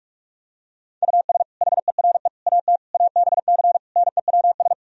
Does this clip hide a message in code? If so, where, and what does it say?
Yes: Morse code from 1.0 to 4.7 s, heard as a keyed tone, about 48 wpm, 700 Hz. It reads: US HERE AT ABC NEWS